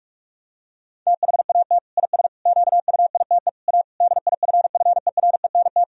Morse code, heard as a keyed tone, about 46 wpm, 700 Hz. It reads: THAT IS QUITE A DIFFERENT
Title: Sentences from Top 1000 Words - Review Part 3 - 45wpm